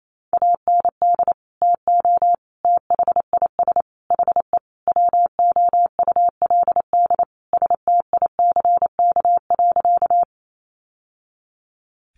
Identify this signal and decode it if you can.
Morse code, heard as a keyed tone, about 28 wpm, 700 Hz. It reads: AND TO THIS HE WOULD STICK.
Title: Sentences from Top 1200 Words 28wpm